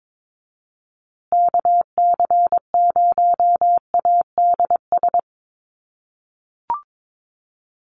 Morse code, heard as a keyed tone, about 22 wpm, 700 Hz. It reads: KC0ADS
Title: USA Call Signs Repeated 2x 22wpm